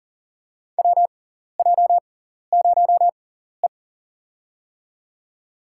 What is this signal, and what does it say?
Morse code, heard as a keyed tone, about 40 wpm, spaced as for 14 wpm, 700 Hz. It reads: WJ0E
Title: USA Call Signs - 4 Characters - 6x Character Spacing - 40wpm